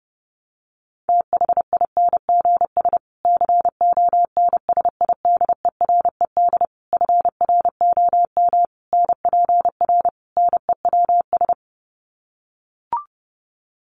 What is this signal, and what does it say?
Morse code, heard as a keyed tone, about 30 wpm, 700 Hz. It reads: THINGS CONSIDERED FROM NPR NEWS
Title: Sets of 5 Words Encore 30wpm